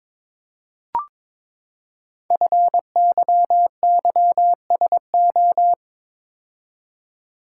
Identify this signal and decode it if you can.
Morse code, heard as a keyed tone, about 22 wpm, 700 Hz. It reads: FYYSO
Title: Code Groups - 5 Letters - 22wpm